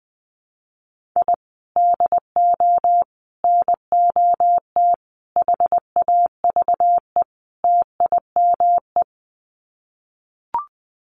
Morse code, heard as a keyed tone, about 20 wpm, 700 Hz. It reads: I DO NOT HAVE TIME
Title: Sets of 5 Words Encore 20wpm